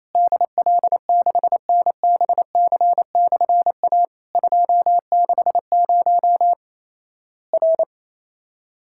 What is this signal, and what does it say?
Morse code, heard as a keyed tone, about 27 wpm, 700 Hz. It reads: DL6NBC/A 260 R
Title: Worked All Europe DX Contest - QTC Exchange - 28wpm